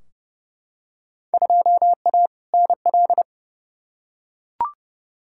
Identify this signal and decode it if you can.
Morse code, heard as a keyed tone, about 30 wpm, 700 Hz. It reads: 2A NL
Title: Field Day Exchanges Encore - 30wpm